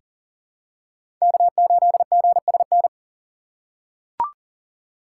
Morse code, heard as a keyed tone, about 40 wpm, 700 Hz. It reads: K8GSN
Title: USA Call Signs - 5 Characters - 4x Character Spacing - 40wpm